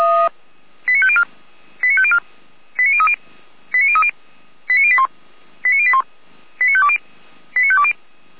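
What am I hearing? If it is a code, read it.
Five-tone selective call (ZVEI), heard as quick runs of five tones, tones of about 70 ms. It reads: 89504 89504 89030 89030 89081 89081 89520 89520
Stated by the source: ZVEI five-tone call received by SDR.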